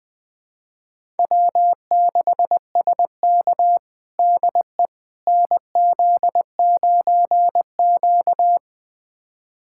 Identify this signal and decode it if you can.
Morse code, heard as a keyed tone, about 20 wpm, 700 Hz. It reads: W6SK DE NZ9Q
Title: USA Callsign DE Callsign - 4 Characters - 20wpm